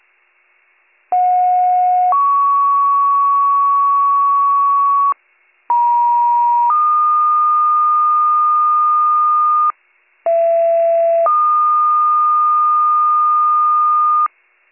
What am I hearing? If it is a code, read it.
Two-tone page, heard as a one-second tone, then a three-second tone, about 730 then 1120 Hz, about 950 then 1220 Hz, about 670 then 1220 Hz. It is heard from 1.1 to 14.3 s.